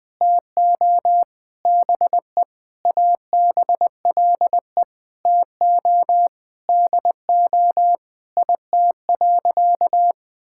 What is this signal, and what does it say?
Morse code, heard as a keyed tone, about 20 wpm, 700 Hz. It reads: TO BE ABLE TO DO IT.
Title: Sentences with 4-Word Phrases - 20wpm